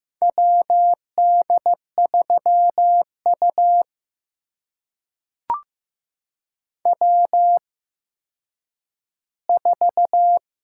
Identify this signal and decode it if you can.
Morse code, heard as a keyed tone, about 15 wpm, spaced as for 9 wpm, 700 Hz. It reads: WD3U W4
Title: USA Call Signs - 4 Characters - 8x Character Spacing - 15wpm